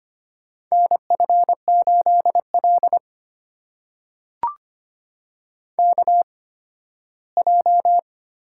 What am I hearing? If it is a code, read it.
Morse code, heard as a keyed tone, about 25 wpm, spaced as for 15 wpm, 700 Hz. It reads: NF8L KJ
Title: USA Call Signs - 4 Characters - 8x Character Spacing - 25wpm